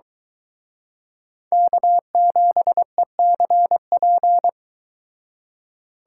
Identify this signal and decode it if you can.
Morse code, heard as a keyed tone, about 22 wpm, 700 Hz. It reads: K7ECP